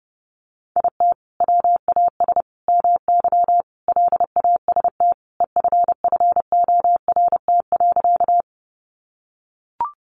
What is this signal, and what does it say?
Morse code, heard as a keyed tone, about 30 wpm, 700 Hz. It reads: IT WAS MY LAST EFFORT.
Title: Sentences from Top 1200 Words 30wpm